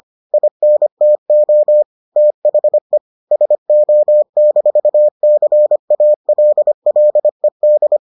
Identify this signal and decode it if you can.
Morse code, heard as a keyed tone, about 25 wpm, 600 Hz. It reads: INTO THE SO-CALLED